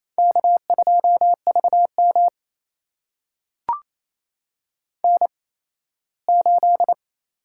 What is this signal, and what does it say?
Morse code, heard as a keyed tone, about 28 wpm, spaced as for 17 wpm, 700 Hz. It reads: K2VM N8
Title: USA Call Signs - 4 Characters - 8x Character Spacing - 28wpm